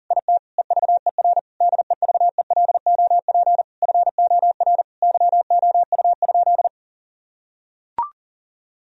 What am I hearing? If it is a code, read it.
Morse code, heard as a keyed tone, about 37 wpm, 700 Hz. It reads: IT EVER DEVELOP FOR YOU?
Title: Sentences from Top 1000 Words - Review Part 2 - 40wpm